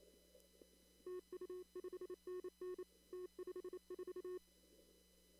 Morse code, heard as a keyed tone, about 28 wpm, 360 Hz. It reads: TU 5NN T54
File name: CW / kt5j wpx cw-op-k5pi